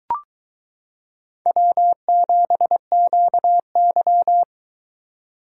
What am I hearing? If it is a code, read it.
Morse code, heard as a keyed tone, about 22 wpm, 700 Hz. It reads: W7QY